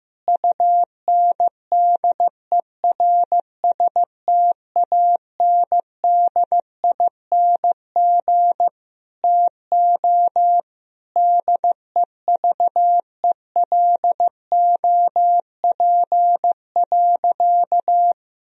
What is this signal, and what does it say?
Morse code, heard as a keyed tone, about 15 wpm, 700 Hz. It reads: UNDERSTANDING TO DEVELOP.
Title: Sentences from Top 1000 Words - Review Part 3 - 15wpm